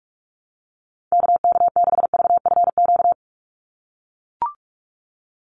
Morse code, heard as a keyed tone, about 44 wpm, 700 Hz. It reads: KK6VFQ
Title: USA Call Signs 45wpm